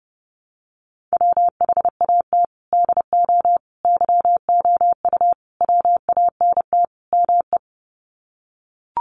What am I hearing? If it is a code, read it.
Morse code, heard as a keyed tone, about 30 wpm, 700 Hz. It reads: WHAT DO YOU WANT ME
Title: Sets of 5 Words Encore2 30wpm